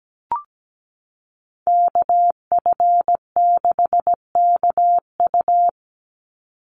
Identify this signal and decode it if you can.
Morse code, heard as a keyed tone, about 17 wpm, 700 Hz. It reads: KF6KU
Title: USA Call Signs 17wpm